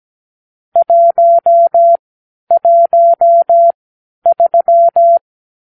Morse code, heard as a keyed tone, about 17 wpm, spaced as for 11 wpm, 680 Hz. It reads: 113
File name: k7qo / Lesson 46 (1-4)